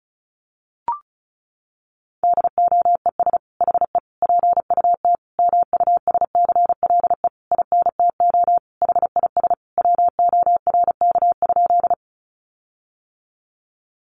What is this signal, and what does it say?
Morse code, heard as a keyed tone, about 34 wpm, 700 Hz. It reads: DOES HE PUT MUSCLE INTO HIS WORK?
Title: Sentences from Top 1300 Words 35wpm